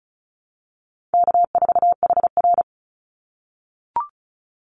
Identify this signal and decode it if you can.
Morse code, heard as a keyed tone, about 36 wpm, 700 Hz. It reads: K4HR
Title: Call Signs - 35wpm